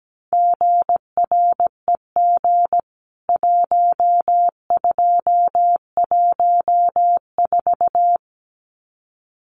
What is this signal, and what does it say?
Morse code, heard as a keyed tone, about 17 wpm, 700 Hz. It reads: GREG 1214
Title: CWT Exchange 17wpm